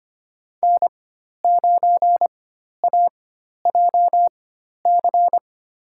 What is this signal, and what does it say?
Morse code, heard as a keyed tone, about 25 wpm, spaced as for 12 wpm, 700 Hz. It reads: N9AJC
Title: USA Call Signs - 4x Character Spacing - 25wpm